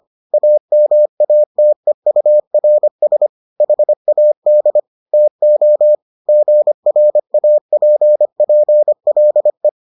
Morse code, heard as a keyed tone, about 25 wpm, 600 Hz. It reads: AMATEURS HAD TO GRAPPLE